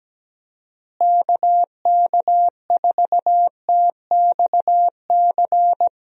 Morse code, heard as a keyed tone, about 17 wpm, 700 Hz. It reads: KK4TXC
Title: USA Call Signs - 6 Characters - Repeated 3x - 17wpm